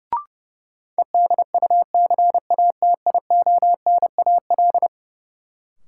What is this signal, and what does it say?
Morse code, heard as a keyed tone, about 32 wpm, 700 Hz. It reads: EDUCATIONAL
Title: Sentences from Top 1200 Words - New Word Drill - 30wpm